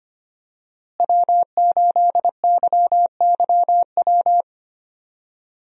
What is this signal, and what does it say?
Morse code, heard as a keyed tone, about 25 wpm, 700 Hz. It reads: W8YYW
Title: USA Call Signs - 5 Characters - Repeated 3x - 25wpm